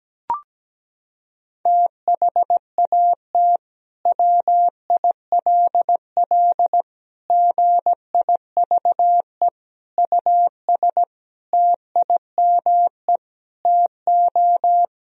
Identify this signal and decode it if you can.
Morse code, heard as a keyed tone, about 17 wpm, 700 Hz. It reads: THAT WILL GIVE US TIME TO
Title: Sentences from Top 1000 Words - Review Part 1 - 17wpm